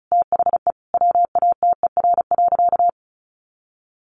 Morse code, heard as a keyed tone, about 34 wpm, 700 Hz. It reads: THE WATER.
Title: Sentences from Top 500 Words 35wpm